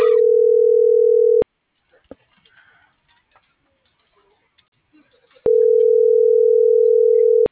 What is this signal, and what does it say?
Ringback tone (North America), 440 + 480 Hz.